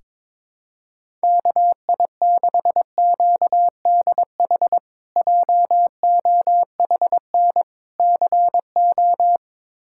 Morse code, heard as a keyed tone, about 22 wpm, 700 Hz. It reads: KI6QDH JOHN CO
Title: K1USN SST QSO Practice - 22wpm